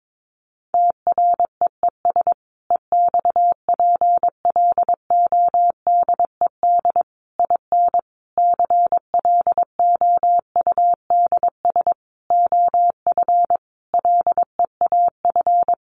Morse code, heard as a keyed tone, about 22 wpm, 700 Hz. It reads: TREES EXPLODED IN CLOUDS OF LEAF
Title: A Sound Of Thunder 22wpm